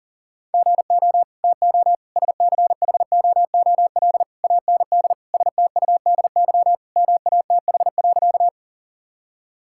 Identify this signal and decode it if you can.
Morse code, heard as a keyed tone, about 40 wpm, 700 Hz. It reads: GO TO SCHOOL AND STUDY MATH.
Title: Sentences from Top 1000 Words - Review Part 6 - 40wpm